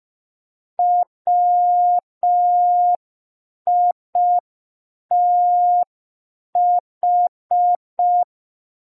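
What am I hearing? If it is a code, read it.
Morse code, heard as a keyed tone, about 5 wpm, 710 Hz. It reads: WITH
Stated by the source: Morse receiving practice lesson for an amateur radio Foundation assessment.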